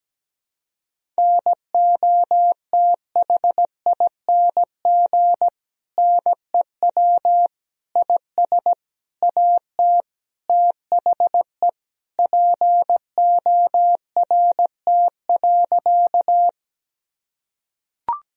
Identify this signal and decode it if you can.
Morse code, heard as a keyed tone, about 17 wpm, 700 Hz. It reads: NOTHING NEW IS AT THE PORT.
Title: Sentences from Top 1000 Words - Review Part 1 - 17wpm